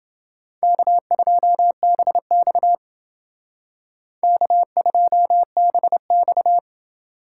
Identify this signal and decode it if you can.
Morse code, heard as a keyed tone, about 29 wpm, 700 Hz. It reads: K2BX K2BX